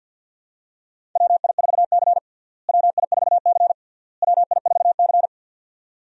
Morse code, heard as a keyed tone, about 47 wpm, 700 Hz. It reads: WI4C WI4C WI4C
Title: USA Call Signs - 4 Characters - Repeated 3x - 50wpm